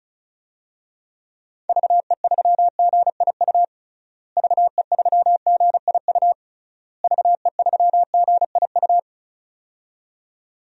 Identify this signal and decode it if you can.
Morse code, heard as a keyed tone, about 36 wpm, 700 Hz. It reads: VE3GIU VE3GIU VE3GIU
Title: DXCC-100 Call Signs - 6 Characters - Repeated 3x - 35wpm